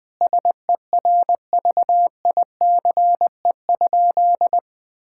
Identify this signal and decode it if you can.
Morse code, heard as a keyed tone, about 20 wpm, 700 Hz. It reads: SERVICE?
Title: Sentences from Top 1000 Words - Review Part 5 - 20wpm